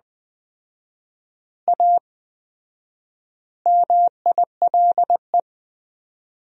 Morse code, heard as a keyed tone, about 20 wpm, 700 Hz. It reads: A MILE
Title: 4x Word-Spacing - Sentences from Top 500 Words - 20wpm